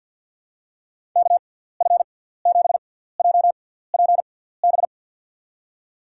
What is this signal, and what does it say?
Morse code, heard as a keyed tone, about 53 wpm, spaced as for 17 wpm, 700 Hz. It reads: KF7JPB
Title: USA Call Signs - 6 Characters - 6x Character Spacing - 50wpm